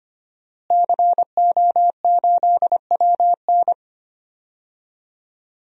Morse code, heard as a keyed tone, about 25 wpm, 700 Hz. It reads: CO8WN
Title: DXCC-100 Call Signs - 5 Characters - Repeated 2x - 25wpm